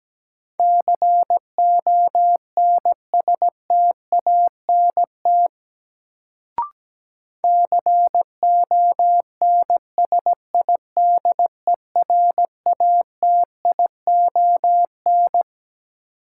Morse code, heard as a keyed tone, about 17 wpm, 700 Hz. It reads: CONSTANT CONSIDERATION
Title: Sentences from Top 1500 Words - New Word Drill Alt - 17wpm